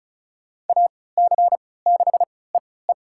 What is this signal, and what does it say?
Morse code, heard as a keyed tone, about 36 wpm, spaced as for 20 wpm, 700 Hz. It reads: AC6EE
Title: USA Call Signs - 5 Characters - 3x Character Spacing - 35wpm